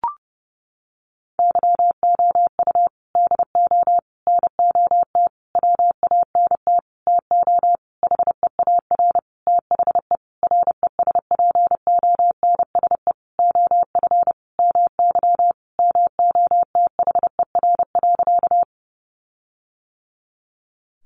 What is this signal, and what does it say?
Morse code, heard as a keyed tone, about 30 wpm, 700 Hz. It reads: YOU DO NOT WANT TO HEAR THE RESPONSE OF MY MOTHER.
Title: Sentences from Top 1400 Words 30wpm